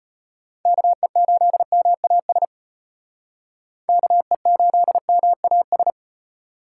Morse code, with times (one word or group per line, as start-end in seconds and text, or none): 0.7-2.5 KE8MAS
3.9-5.9 KE8MAS